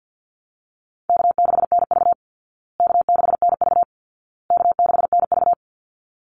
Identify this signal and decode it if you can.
Morse code, heard as a keyed tone, about 50 wpm, 700 Hz. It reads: K6NV K6NV K6NV
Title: Easy Call Signs Repeated 3x - 50wpm